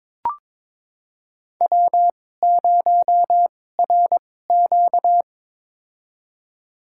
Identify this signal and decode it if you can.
Morse code, heard as a keyed tone, about 22 wpm, spaced as for 16 wpm, 700 Hz. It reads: W0RQ
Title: USA Call Signs - 4 Characters - 2x Character Spacing - 22wpm